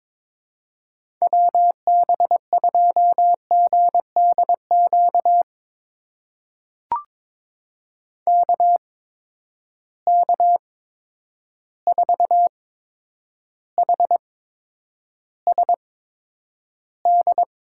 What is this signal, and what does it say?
Morse code, heard as a keyed tone, about 22 wpm, spaced as for 9 wpm, 700 Hz. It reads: WB2GDQ KK4HSD
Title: USA Call Signs - 8x Character Spacing - 22wpm